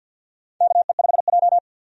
Morse code, heard as a keyed tone, about 51 wpm, 700 Hz. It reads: KE5J
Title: USA Call Signs - 4 Characters - 50wpm